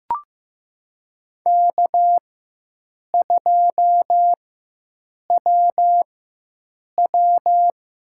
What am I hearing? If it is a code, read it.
Morse code, heard as a keyed tone, about 15 wpm, spaced as for 7 wpm, 700 Hz. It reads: K2WW
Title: USA Call Signs - 4x Character Spacing - 15wpm